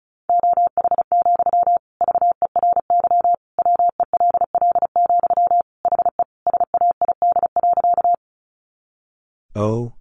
Morse code, heard as a keyed tone, about 34 wpm, 700 Hz. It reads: OH, VERY WELL, HE SAID.
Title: Sentences from Top 500 Words 35wpm